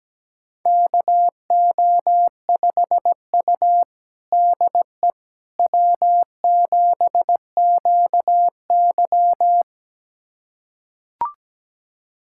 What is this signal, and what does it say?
Morse code, heard as a keyed tone, about 17 wpm, 700 Hz. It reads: KO5U DE W7QY
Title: USA Callsign DE Callsign - 4 Characters - 17wpm